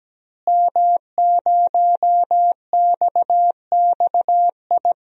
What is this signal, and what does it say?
Morse code, heard as a keyed tone, about 17 wpm, 700 Hz. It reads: M0XXI
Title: DXCC-100 Call Signs - 5 Characters - Repeated 3x - 17wpm